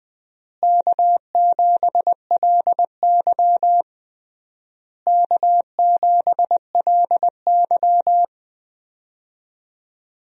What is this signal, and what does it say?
Morse code, heard as a keyed tone, about 20 wpm, 700 Hz. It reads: K7LY K7LY
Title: USA Call Signs - 4 Characters - Repeated 3x - 20wpm